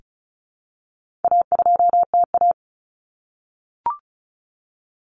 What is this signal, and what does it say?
Morse code, heard as a keyed tone, about 34 wpm, 700 Hz. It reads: A2TA
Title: Common Cut Numbers - 4-Digit 35wpm